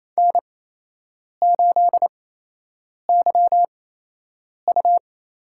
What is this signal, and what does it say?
Morse code, heard as a keyed tone, about 28 wpm, spaced as for 8 wpm, 700 Hz. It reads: N8YU
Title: USA Call Signs - 8x Character Spacing - 28wpm